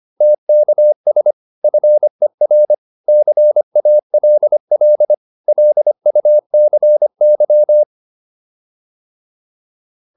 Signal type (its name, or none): Morse code